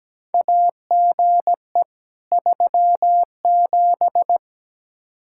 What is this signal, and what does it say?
Morse code, heard as a keyed tone, about 17 wpm, 700 Hz. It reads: AGE 37
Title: K1USN SST QSO Practice 17wpm